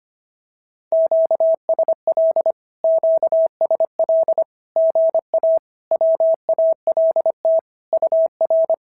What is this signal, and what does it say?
Morse code, heard as a keyed tone, about 25 wpm, 650 Hz. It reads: QSL QSL GA WALT UR
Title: SOTA QSO Practice 25wpm